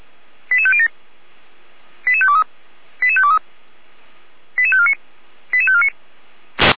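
Five-tone selective call (ZVEI), heard as quick runs of five tones, tones of about 70 ms. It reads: 88507 88523 88523 88549 88549
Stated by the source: ZVEI five-tone call received by SDR.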